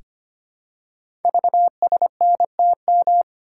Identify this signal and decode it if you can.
Morse code, heard as a keyed tone, about 25 wpm, 700 Hz. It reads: VSNTM
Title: Code Groups - 5 Letter:Numbers - 25wpm